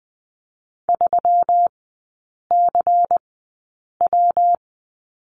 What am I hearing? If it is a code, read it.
Morse code, heard as a keyed tone, about 20 wpm, spaced as for 8 wpm, 700 Hz. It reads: 3CW
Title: 133 Lesson 22e - Question Mark - Three Characters 20wpm